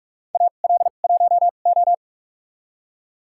Morse code, heard as a keyed tone, about 46 wpm, spaced as for 32 wpm, 700 Hz. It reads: AL1O
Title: USA Call Signs - 4 Characters - 2x Character Spacing - 45wpm